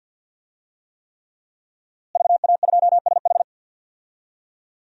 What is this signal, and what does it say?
Morse code, heard as a keyed tone, about 47 wpm, 700 Hz. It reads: VA2SH
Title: DXCC-100 Call Signs - 5 Characters - Repeated 3x - 50wpm